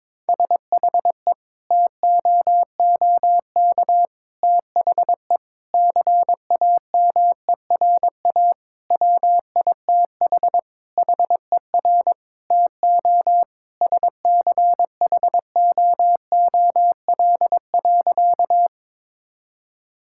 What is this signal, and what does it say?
Morse code, heard as a keyed tone, about 22 wpm, 700 Hz. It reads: SHE TOOK THE CAMERA WITH HER TO SCHOOL.
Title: Sentences from Top 1000 Words - Review Part 5 - 22wpm